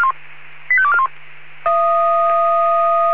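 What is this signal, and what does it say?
Five-tone selective call (ZVEI), heard as quick runs of five tones, tones of about 70 ms. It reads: 85241
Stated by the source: ZVEI five-tone call 85241 received by SDR.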